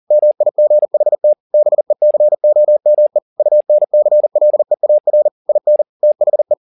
Morse code, heard as a keyed tone, about 40 wpm, 600 Hz. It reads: MIGHT BECOME UNCLEAR IN THE
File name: the old man and the sea 2 40wpm